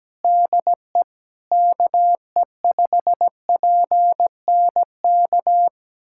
Morse code, heard as a keyed tone, about 17 wpm, 700 Hz. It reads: DE KE5PNK
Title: USA Callsign DE Callsign - 6 Characters - 17wpm